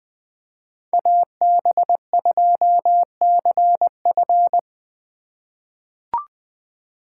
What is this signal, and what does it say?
Morse code, heard as a keyed tone, about 21 wpm, 700 Hz. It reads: AB2CF
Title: USA Call Signs - 5 Characters - 3x Character Spacing - 20wpm